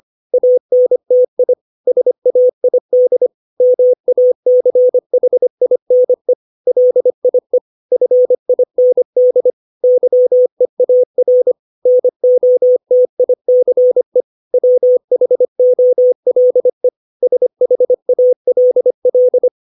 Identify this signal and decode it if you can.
Morse code, heard as a keyed tone, about 25 wpm, 500 Hz. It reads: ANTI SAID MACHINE LIE FIND YEAR NOTICE WHOLE SHALL
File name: 632 words / 632 Words 25WPM